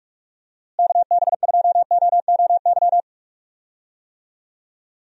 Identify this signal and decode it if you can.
Morse code, heard as a keyed tone, about 46 wpm, 700 Hz. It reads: KB2OOY
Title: USA Call Signs - 6 Characters - Repeated 2x - 45wpm